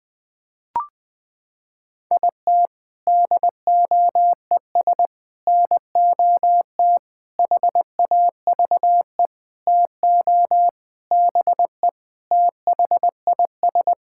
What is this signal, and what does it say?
Morse code, heard as a keyed tone, about 20 wpm, 700 Hz. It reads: IT DOES NOT HAVE TO BE THIS
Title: Sentences with 5-Word Phrases - 20wpm